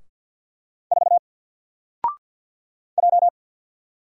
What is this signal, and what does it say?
Morse code, heard as a keyed tone, about 48 wpm, 700 Hz. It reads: 4 J
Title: Single Character (Letter:Number:Punctuation:Prosign) 50wpm